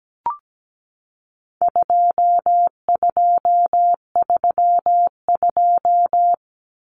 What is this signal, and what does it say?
Morse code, heard as a keyed tone, about 17 wpm, 700 Hz. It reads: 2232
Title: Common Cut Numbers - 4-Digit 17wpm